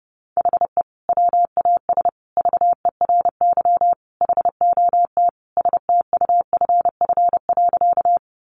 Morse code, heard as a keyed tone, about 30 wpm, 700 Hz. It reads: HE WAS VERY HOT STUFF.
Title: Sentences from Top 1400 Words 30wpm